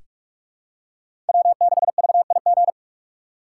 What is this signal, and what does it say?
Morse code, heard as a keyed tone, about 46 wpm, 700 Hz. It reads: W6VIG
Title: USA Call Signs - 5 Characters - 2x Character Spacing - 45wpm